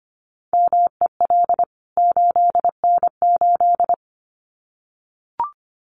Morse code, heard as a keyed tone, about 25 wpm, 700 Hz. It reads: MEL 8N8
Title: CWT Exchange 25wpm